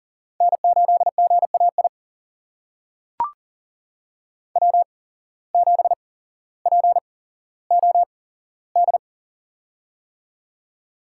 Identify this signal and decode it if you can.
Morse code, heard as a keyed tone, about 40 wpm, spaced as for 17 wpm, 700 Hz. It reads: N8GAI W7POD
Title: USA Call Signs - 5 Characters - 8x Character Spacing - 40wpm